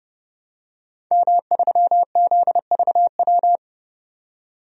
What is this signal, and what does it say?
Morse code, heard as a keyed tone, about 30 wpm, 700 Hz. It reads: M3ZVW
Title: DXCC-100 Call Signs - 5 Characters - 30wpm